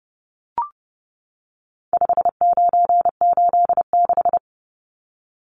Morse code, heard as a keyed tone, about 30 wpm, 700 Hz. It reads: 5986
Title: Numbers - 4-Digit 30wpm